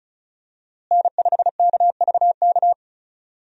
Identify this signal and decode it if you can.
Morse code, heard as a keyed tone, about 36 wpm, 700 Hz. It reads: N5KVK
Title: USA Call Signs - 5 Characters - Repeated 3x - 35wpm